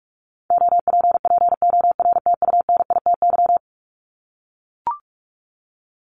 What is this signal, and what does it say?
Morse code, heard as a keyed tone, about 44 wpm, 700 Hz. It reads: OPPORTUNITY
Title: Top 751-1000 Words 45wpm